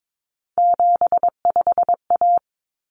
Morse code, heard as a keyed tone, about 22 wpm, 700 Hz. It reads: 75A
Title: Common Cut Numbers - 3-Digit 22wpm